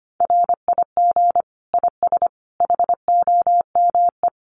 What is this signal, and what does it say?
Morse code, heard as a keyed tone, about 25 wpm, 700 Hz. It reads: RIG IS HOME
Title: Common QSO Phrases 25wpm